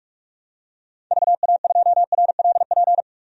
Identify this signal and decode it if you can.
Morse code, heard as a keyed tone, about 46 wpm, 700 Hz. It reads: VA2RLP
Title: DXCC-100 Call Signs - 6 Characters - 45wpm